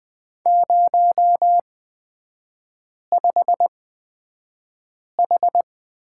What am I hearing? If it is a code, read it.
Morse code, heard as a keyed tone, about 20 wpm, spaced as for 5 wpm, 700 Hz. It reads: 05H
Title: 229 Lesson 37e - Letter x - Three Characters 20wpm